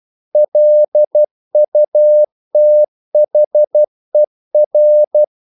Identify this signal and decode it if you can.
Morse code, heard as a keyed tone, about 12 wpm, 600 Hz. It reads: LUTHER
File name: CW / Practice / EnglishQuotations0312